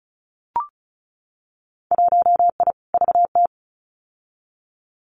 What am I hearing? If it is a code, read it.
Morse code, heard as a keyed tone, about 34 wpm, 700 Hz. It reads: 1I VT
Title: Winter Field Day Exchanges Encore 35wpm